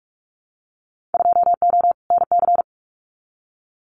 Morse code, heard as a keyed tone, about 44 wpm, 700 Hz. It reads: 2O NC
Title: Winter Field Day Exchanges 45wpm